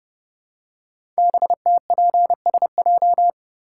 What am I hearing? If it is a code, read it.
Morse code, heard as a keyed tone, about 30 wpm, 700 Hz. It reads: BTPSJ
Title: Code Groups - 5 Letters - 30wpm